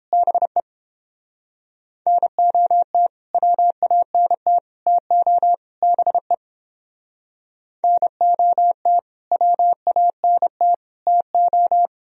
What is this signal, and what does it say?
Morse code, heard as a keyed tone, about 31 wpm, 700 Hz. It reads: BE NOT WANT TO BE NOT WANT TO